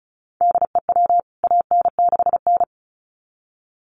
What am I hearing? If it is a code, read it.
Morse code, heard as a keyed tone, about 34 wpm, 700 Hz. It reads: DEW AN6N
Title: CWT Exchange 35wpm